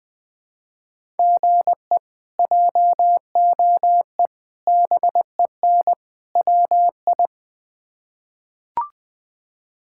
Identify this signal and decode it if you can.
Morse code, heard as a keyed tone, about 21 wpm, 700 Hz. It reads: GE JOE BEN WI
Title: K1USN SST QSO Practice - 20wpm